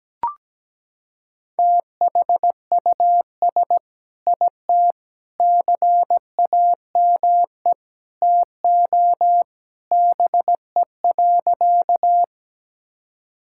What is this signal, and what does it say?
Morse code, heard as a keyed tone, about 17 wpm, 700 Hz. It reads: THUS IT CAME TO BE.
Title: Sentences from Top 1000 Words - Review Part 5 - 17wpm